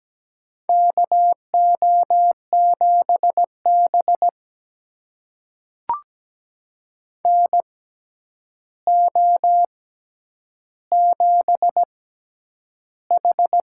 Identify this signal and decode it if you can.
Morse code, heard as a keyed tone, about 17 wpm, spaced as for 9 wpm, 700 Hz. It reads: KO7B NO7H
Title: USA Call Signs - 6x Character Spacing - 17wpm